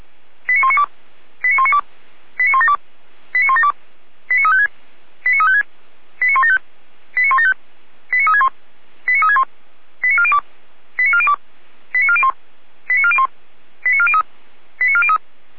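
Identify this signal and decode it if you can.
Five-tone selective call (ZVEI), heard as quick runs of five tones, tones of about 70 ms. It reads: 89192 89192 89172 89172 89356 89356 89175 89175 89361 89361 89402 89402 89401 89401 89403 89403
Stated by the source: ZVEI five-tone selective call received on an SDR.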